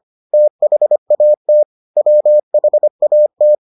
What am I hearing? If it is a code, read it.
Morse code, heard as a keyed tone, about 25 wpm, 600 Hz. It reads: THAT WHAT